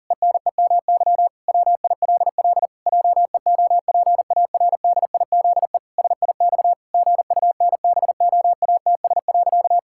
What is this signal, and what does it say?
Morse code, heard as a keyed tone, about 40 wpm, 700 Hz. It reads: ENEMY WILL JEOPARDIZE SIX GUNBOATS.
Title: Pangram Challenge v2 - 40wpm